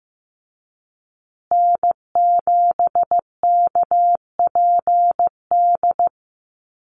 Morse code, heard as a keyed tone, about 15 wpm, 700 Hz. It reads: N7KPD
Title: USA Call Signs Repeated 3x 15wpm